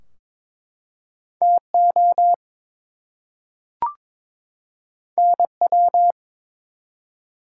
Morse code, heard as a keyed tone, about 22 wpm, 700 Hz. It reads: TO NW
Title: Top 180 QSO Elements 22wpm